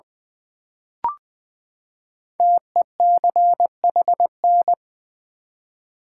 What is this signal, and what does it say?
Morse code, heard as a keyed tone, about 20 wpm, 700 Hz. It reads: TECHN
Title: Build A Word - 5 Letters At A Time - 10-Letter Words - 20wpm